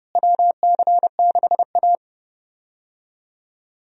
Morse code, heard as a keyed tone, about 32 wpm, 700 Hz. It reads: WC6A
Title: USA Call Signs - 4 Characters - Repeated 3x - 30wpm